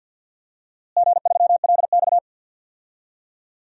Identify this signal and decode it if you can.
Morse code, heard as a keyed tone, about 52 wpm, 700 Hz. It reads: G3LX
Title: DXCC-100 Call Signs - 4 Characters - Repeated 3x - 50wpm